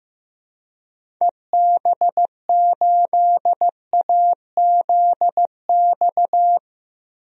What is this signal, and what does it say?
Morse code, heard as a keyed tone, about 16 wpm, 700 Hz. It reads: EB8AZX
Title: DXCC-100 Call Signs - 6 Characters - Repeated 2x - 15wpm